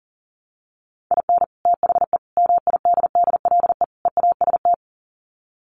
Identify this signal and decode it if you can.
Morse code, heard as a keyed tone, about 40 wpm, 700 Hz. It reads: IN THE MIDDLE EAST
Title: Sets of 4 Words 40wpm